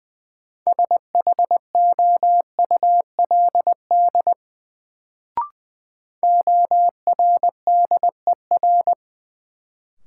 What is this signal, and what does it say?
Morse code, heard as a keyed tone, about 20 wpm, 700 Hz. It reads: SHOULD ORDER
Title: Sets of 2 Words - New Word Drill - 20wpm